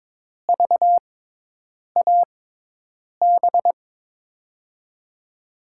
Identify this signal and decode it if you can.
Morse code, heard as a keyed tone, about 23 wpm, spaced as for 8 wpm, 700 Hz. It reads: VAB
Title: USA Call Signs - 5 Characters - 6x Character Spacing - 22wpm